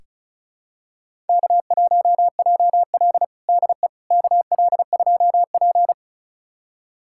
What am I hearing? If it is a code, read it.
Morse code, heard as a keyed tone, about 34 wpm, 700 Hz. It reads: K1JL DE KL2P
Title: USA Callsign DE Callsign - 4 Characters - 35wpm